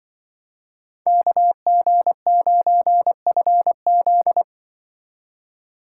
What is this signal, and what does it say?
Morse code, heard as a keyed tone, about 24 wpm, 700 Hz. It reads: KG9FZ
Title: Speed Racing - USA 2x2 Call Signs 20wpm